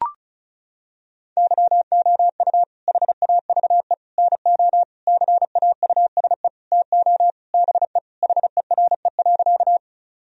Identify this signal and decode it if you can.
Morse code, heard as a keyed tone, about 36 wpm, 700 Hz. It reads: YOU HAVE NO CAUSE TO BE HERE.
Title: Sentences from Top 1000 Words - Review Part 1 - 35wpm